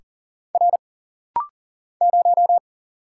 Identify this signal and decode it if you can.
Morse code, heard as a keyed tone, about 40 wpm, 700 Hz. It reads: R 0
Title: Single Letter-Number - Rapid-Fire 40wpm